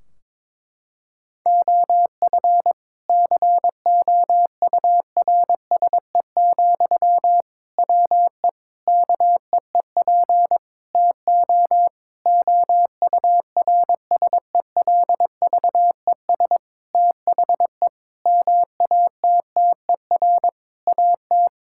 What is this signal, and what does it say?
Morse code, heard as a keyed tone, about 22 wpm, 700 Hz. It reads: OF COURSE, WE KEEP TO OURSELVES THE MATTER AT
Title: Sentences from Top 1600 Words 22wpm